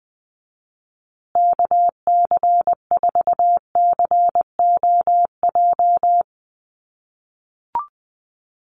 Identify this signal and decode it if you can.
Morse code, heard as a keyed tone, about 20 wpm, 700 Hz. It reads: KC4COJ
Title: USA Call Signs Repeated 2x 20wpm